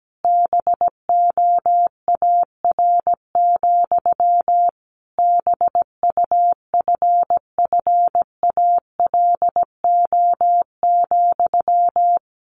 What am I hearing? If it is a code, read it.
Morse code, heard as a keyed tone, about 17 wpm, 700 Hz. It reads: BOAR, BUFFALO,
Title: A Sound Of Thunder 17wpm